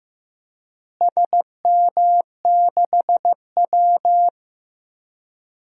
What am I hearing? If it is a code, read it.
Morse code, heard as a keyed tone, about 15 wpm, 700 Hz. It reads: SM6W
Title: DXCC-100 Call Signs - 4 Characters - Repeated 3x - 15wpm